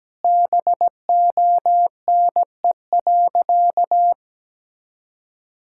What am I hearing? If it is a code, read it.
Morse code, heard as a keyed tone, about 17 wpm, 700 Hz. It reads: BONE.
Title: Sentences from Top 1600 Words 17wpm